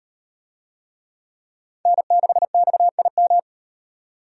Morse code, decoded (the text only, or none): N6XIM